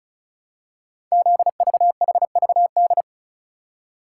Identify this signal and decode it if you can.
Morse code, heard as a keyed tone, about 36 wpm, 700 Hz. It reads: ZVHVD
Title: Code Groups - 5 Letters - 35wpm